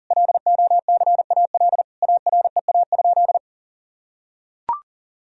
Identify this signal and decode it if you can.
Morse code, heard as a keyed tone, about 40 wpm, 700 Hz. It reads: LOCAL AREA?
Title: Sentences from Top 1000 Words - Review Part 3 - 40wpm